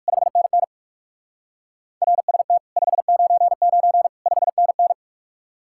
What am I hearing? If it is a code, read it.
Morse code, heard as a keyed tone, about 59 wpm, 700 Hz. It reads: 5NN RST 599 5NN